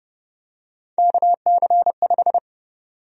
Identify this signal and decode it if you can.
Morse code, heard as a keyed tone, about 30 wpm, 700 Hz. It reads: KC5
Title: USA Call Sign Prefixes - Repeated 2x - 30wpm